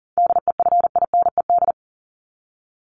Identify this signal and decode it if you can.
Morse code, heard as a keyed tone, about 40 wpm, 700 Hz. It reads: DEFINED
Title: ED Suffix - Top 100 Words with Base Words 40wpm